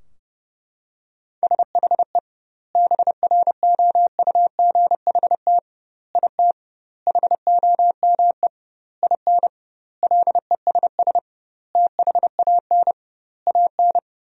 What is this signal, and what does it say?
Morse code, heard as a keyed tone, about 30 wpm, 700 Hz. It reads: SHE BROUGHT IT HOME IN LESS THAN AN
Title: 2x Word-Spacing - Sentences from Top 500 Words - 30wpm